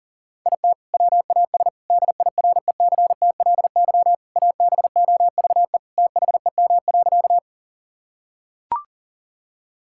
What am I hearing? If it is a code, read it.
Morse code, heard as a keyed tone, about 40 wpm, 700 Hz. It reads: IT WAS DIRECTLY ABOVE THEM.
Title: Sentences from Top 1000 Words - Review Part 5 - 40wpm